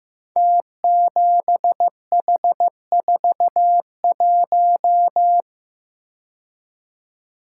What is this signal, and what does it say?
Morse code, heard as a keyed tone, about 15 wpm, 700 Hz. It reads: T7H41
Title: Code Groups - 5 Letter:Numbers - 15wpm